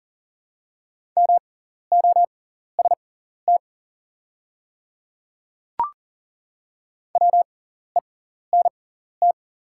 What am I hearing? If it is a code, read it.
Morse code, heard as a keyed tone, about 39 wpm, spaced as for 14 wpm, 690 Hz. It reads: MOST WENT
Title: 6x Character Spacing - Top 200 Words - 40wpm